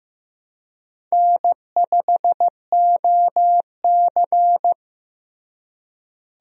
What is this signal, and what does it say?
Morse code, heard as a keyed tone, about 15 wpm, 700 Hz. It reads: N5OC
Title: USA Call Signs - 4 Characters - Repeated 2x - 15wpm